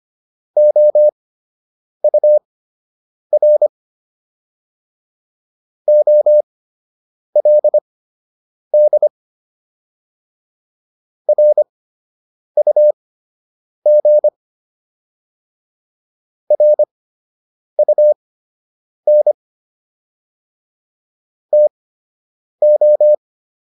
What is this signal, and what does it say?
Morse code, heard as a keyed tone, about 25 wpm, spaced as for 8 wpm, 600 Hz. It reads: OUR OLD RUG RUN TO